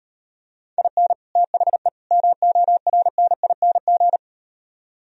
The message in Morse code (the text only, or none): IN THE MORNING